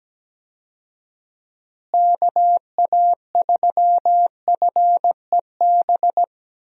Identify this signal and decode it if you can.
Morse code, heard as a keyed tone, about 17 wpm, 700 Hz. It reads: KA3FEB
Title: USA Call Signs - 6 Characters - Repeated 3x - 17wpm